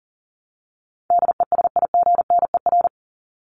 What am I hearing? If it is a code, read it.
Morse code, heard as a keyed tone, about 40 wpm, 700 Hz. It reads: DESIGNER
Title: Top 1501-1750 Words 40wpm